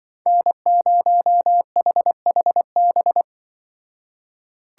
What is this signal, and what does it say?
Morse code, heard as a keyed tone, about 25 wpm, 700 Hz. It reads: N0HHB
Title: Speed Racing - USA 1x3 Call Signs 20wpm